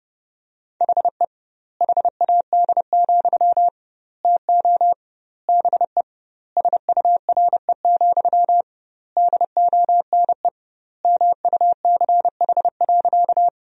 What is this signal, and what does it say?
Morse code, heard as a keyed tone, about 30 wpm, 700 Hz. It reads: HE HAD, TO BE SURE, DONE MUCH.
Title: 2x Word-Spacing - Sentences from Top 500 Words - 30wpm